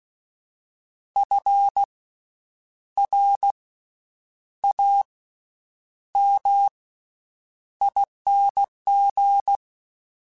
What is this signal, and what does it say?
Morse code, heard as a keyed tone, about 16 wpm, spaced as for 8 wpm, 800 Hz. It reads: FRAMING
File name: Special INGs / Special INGs-mp3s / VerySlowToFaster Ing @15